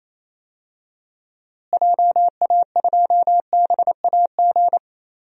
Morse code, heard as a keyed tone, about 28 wpm, 700 Hz. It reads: JA2BAG
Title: DXCC-100 Call Signs - 6 Characters - Repeated 3x - 28wpm